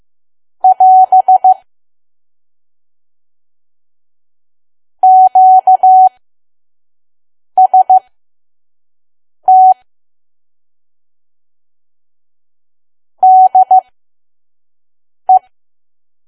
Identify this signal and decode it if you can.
Morse code, heard as a keyed tone, about 15 wpm, spaced as for 5 wpm, 750 Hz. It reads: <AS> QST DE
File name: Archive / 5 WPM files / PM